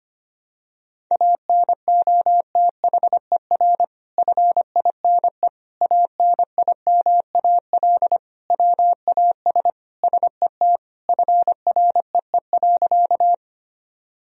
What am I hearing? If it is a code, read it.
Morse code, heard as a keyed tone, about 25 wpm, 700 Hz. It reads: ANOTHER FINE ANIMAL WAS SET FREE.
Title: Sentences from Top 1000 Words - Review Part 1 - 25wpm